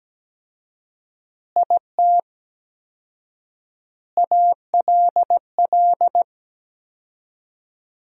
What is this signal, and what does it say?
Morse code, heard as a keyed tone, about 17 wpm, 700 Hz. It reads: IT ALL
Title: 4x Word-Spacing - Sentences from Top 500 Words - 17wpm